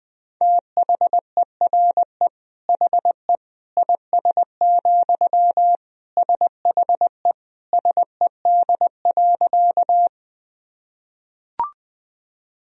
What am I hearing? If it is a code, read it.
Morse code, heard as a keyed tone, about 20 wpm, 700 Hz. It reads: THERE HE IS, SHE SED.
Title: Sentences from Top 100 Words and Abbreviations 20wpm